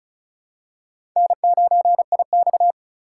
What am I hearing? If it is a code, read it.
Morse code, heard as a keyed tone, about 36 wpm, 700 Hz. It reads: N9IX
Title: USA Call Signs - 4 Characters - 2x Character Spacing - 35wpm